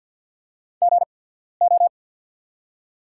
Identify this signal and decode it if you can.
Morse code, heard as a keyed tone, about 48 wpm, spaced as for 14 wpm, 700 Hz. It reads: GO
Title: 8x Character Spacing - Top 100 Words - 50wpm